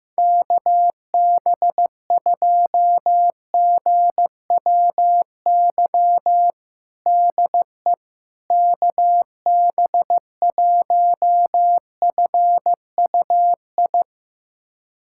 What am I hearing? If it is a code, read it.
Morse code, heard as a keyed tone, about 15 wpm, 700 Hz. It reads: KB2GWY DE KB1FUI
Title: USA Callsign DE Callsign - 6 Characters - 15wpm